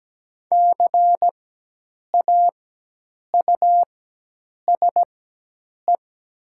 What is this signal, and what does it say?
Morse code, heard as a keyed tone, about 17 wpm, spaced as for 8 wpm, 700 Hz. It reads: CAUSE
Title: 4x Character Spacing - Top 200 Words - 17wpm